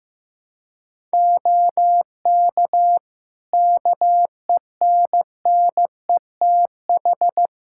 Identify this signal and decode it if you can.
Morse code, heard as a keyed tone, about 15 wpm, 700 Hz. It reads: OK KENNETH
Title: Practice QSO - Exchange 3 - More Info - 15wpm